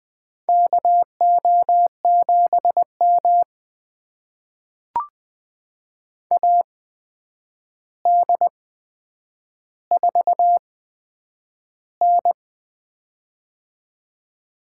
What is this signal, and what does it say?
Morse code, heard as a keyed tone, about 20 wpm, spaced as for 9 wpm, 700 Hz. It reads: KO7M AD4N
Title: USA Call Signs - 4 Characters - 8x Character Spacing - 20wpm